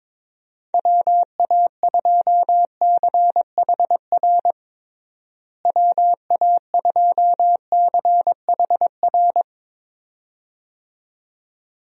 Morse code, heard as a keyed tone, about 22 wpm, 700 Hz. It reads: WA2CHR WA2CHR
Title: USA Call Signs - 6 Characters - Repeated 2x - 22wpm